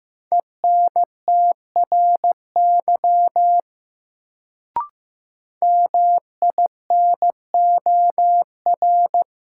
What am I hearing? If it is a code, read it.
Morse code, heard as a keyed tone, about 15 wpm, 700 Hz. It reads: ENTRY MINOR
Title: Sentences from Top 1500 Words - New Word Drill - 15wpm